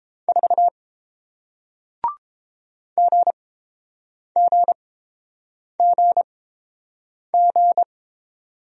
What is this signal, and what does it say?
Morse code, heard as a keyed tone, about 30 wpm, spaced as for 7 wpm, 700 Hz. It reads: 4 GGGG